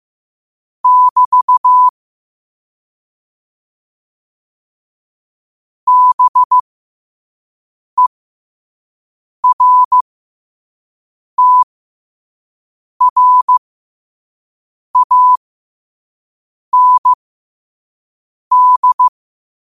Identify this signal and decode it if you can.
Morse code, heard as a keyed tone, about 15 wpm, spaced as for 5 wpm, 1000 Hz. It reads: <BT> BERTRAND
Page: New quotes every day in morse code at 5 Words per minute.